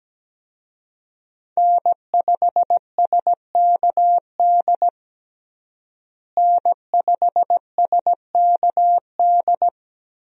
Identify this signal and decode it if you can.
Morse code, heard as a keyed tone, about 17 wpm, 700 Hz. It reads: N5SKD N5SKD
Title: USA Call Signs - 5 Characters - Repeated 2x - 17wpm